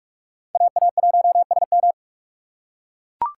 Morse code, heard as a keyed tone, about 44 wpm, 700 Hz. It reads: AA1SM